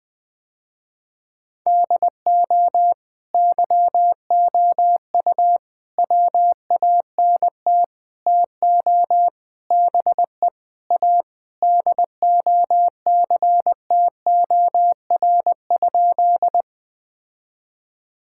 Morse code, heard as a keyed tone, about 20 wpm, 700 Hz. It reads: DO YOU WANT TO BE A DOCTOR?
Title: Sentences with 4-Word Phrases - Spoken Beginning - 20wpm